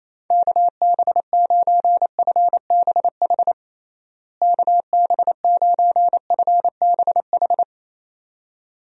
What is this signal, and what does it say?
Morse code, heard as a keyed tone, about 28 wpm, 700 Hz. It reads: KB9FBH KB9FBH
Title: USA Call Signs - 6 Characters - Repeated 2x - 28wpm